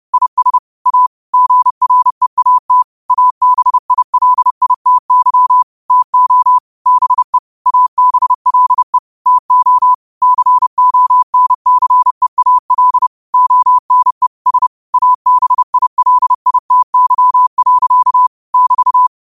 Morse code, heard as a keyed tone, about 30 wpm, 1000 Hz. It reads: IS A GREAT ABILITY TO BE ABLE TO CONCEAL ONE S ABILITY. <BT>
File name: qotdfiles / 30wpm